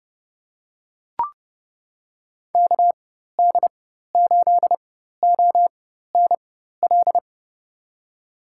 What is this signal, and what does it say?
Morse code, heard as a keyed tone, about 30 wpm, spaced as for 14 wpm, 700 Hz. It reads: KD8ONL
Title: USA Call Signs - 6 Characters - 4x Character Spacing - 30wpm